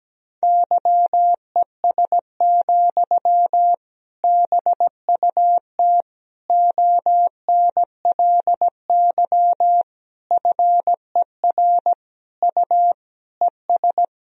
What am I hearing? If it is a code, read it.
Morse code, heard as a keyed tone, about 17 wpm, 700 Hz. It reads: YES, BUT ONLY FER U ES
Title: Sentences from Top 500 Words and Abbreviations 17wpm